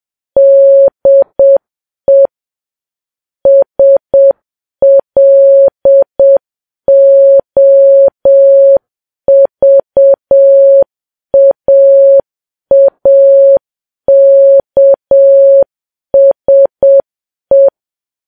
Morse code, heard as a keyed tone, about 7 wpm, 550 Hz. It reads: DE SLOVAAKSE
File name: Files Web tls / Morse on9fv / morse7kps